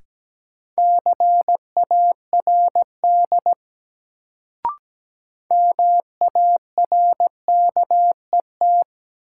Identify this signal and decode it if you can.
Morse code, heard as a keyed tone, about 17 wpm, 700 Hz. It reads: CARD MARKET
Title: Sentences from Top 600 Words - New Word Drill - 17wpm